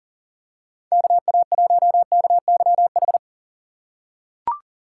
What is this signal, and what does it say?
Morse code, heard as a keyed tone, about 40 wpm, 700 Hz. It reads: KA1KYH
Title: USA Call Signs - 6 Characters - 3x Character Spacing - 40wpm